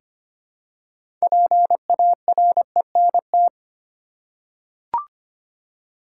Morse code, heard as a keyed tone, about 25 wpm, 700 Hz. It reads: PARENT
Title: Six Letter Words - 25wpm